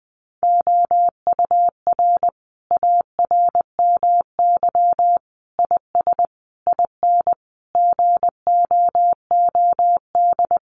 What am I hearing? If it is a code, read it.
Morse code, heard as a keyed tone, about 20 wpm, 700 Hz. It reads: OUR ARMY IS IN GOOD
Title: Sentences from Top 1500 Words 20wpm